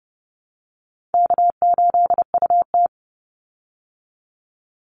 Morse code, heard as a keyed tone, about 30 wpm, 700 Hz. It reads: K8UT
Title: Call Signs Repeated 3x - 30wpm